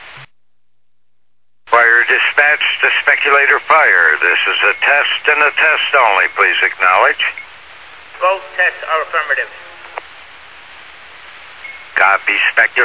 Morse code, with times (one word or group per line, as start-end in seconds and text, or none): none